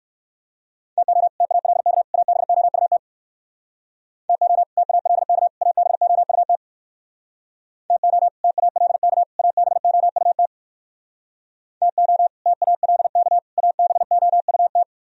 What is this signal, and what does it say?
Morse code, heard as a keyed tone, about 67 wpm, 700 Hz. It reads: TO TALK ABOUT TO TALK ABOUT TO TALK ABOUT TO TALK ABOUT